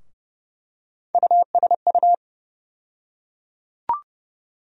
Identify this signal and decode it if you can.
Morse code, heard as a keyed tone, about 32 wpm, 700 Hz. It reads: USU
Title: USA Call Sign Postfixes - 3 Letters 30wpm